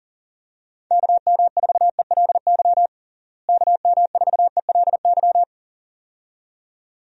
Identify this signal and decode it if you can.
Morse code, heard as a keyed tone, about 40 wpm, 700 Hz. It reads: KM4ELY KM4ELY
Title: USA Call Signs - 6 Characters - Repeated 2x - 40wpm